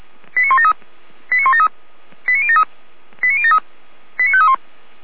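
Five-tone selective call (ZVEI), heard as quick runs of five tones, tones of about 70 ms. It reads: 89173 89173 89073 89073 89531
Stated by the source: ZVEI five-tone call received on SDR, from feuerwehr-orlamuende fire brigade.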